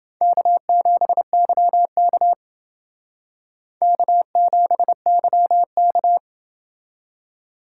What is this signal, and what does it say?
Morse code, heard as a keyed tone, about 28 wpm, 700 Hz. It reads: K7YK K7YK